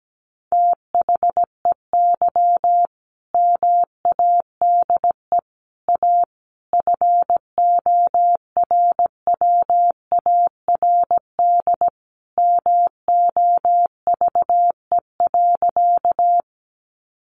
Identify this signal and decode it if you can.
Morse code, heard as a keyed tone, about 17 wpm, 700 Hz. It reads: THEY MADE A FORWARD MOVE.
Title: Sentences from Top 1300 Words 17wpm